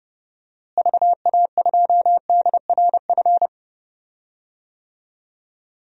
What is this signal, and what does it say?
Morse code, heard as a keyed tone, about 32 wpm, 700 Hz. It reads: VA2DRF
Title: DXCC-100 Call Signs - 6 Characters - Repeated 3x - 30wpm